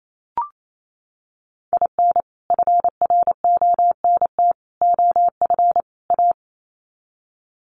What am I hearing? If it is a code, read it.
Morse code, heard as a keyed tone, about 28 wpm, 700 Hz. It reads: IN FRONT OF A
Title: Sets of 4 Words 28wpm